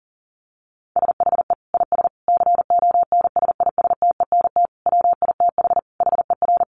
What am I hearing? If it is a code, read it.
Morse code, heard as a keyed tone, about 40 wpm, 700 Hz. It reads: SHE IS CONSISTENT WITH HER
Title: Sentences from Top 1500 Words 40wpm